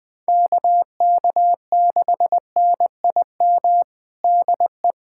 Morse code, heard as a keyed tone, about 21 wpm, 700 Hz. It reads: KK6NIM DE
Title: USA Callsign DE Callsign - 6 Characters - 20wpm